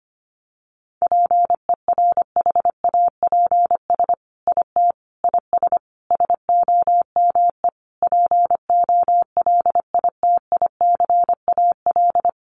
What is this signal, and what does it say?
Morse code, heard as a keyed tone, about 25 wpm, 700 Hz. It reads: PERHAPS IT IS SOME POLITICAL
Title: Sentences from Top 800 Words 25wpm